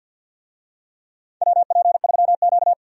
Morse code, heard as a keyed tone, about 50 wpm, 700 Hz. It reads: WP3Q
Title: USA Call Signs - 4 Characters - 50wpm